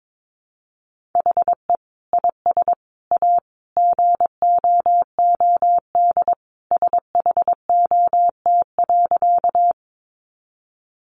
Morse code, heard as a keyed tone, about 22 wpm, 700 Hz. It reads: HE IS A GOOD SHOT.
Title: Sentences from Top 1100 Words 22wpm